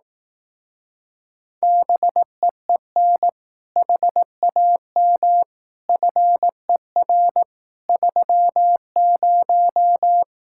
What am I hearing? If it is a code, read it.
Morse code, heard as a keyed tone, about 18 wpm, 700 Hz. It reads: BEEN HAM FER 30
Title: Speed-Racing Common QSO Phrases Encore - 15wpm